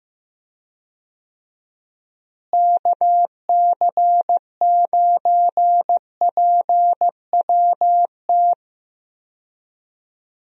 Morse code, heard as a keyed tone, about 16 wpm, 700 Hz. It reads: KC9PWT
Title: USA Call Signs - 6 Characters - Repeated 2x - 15wpm